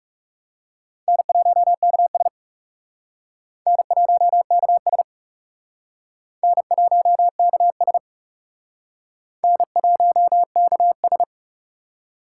Morse code, heard as a keyed tone, about 37 wpm, 700 Hz. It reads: N1KS N1KS N1KS N1KS
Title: Speed Racing - USA 1x2 Call Signs 30wpm